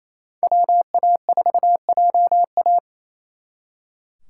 Morse code, heard as a keyed tone, about 36 wpm, 700 Hz. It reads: WA4JA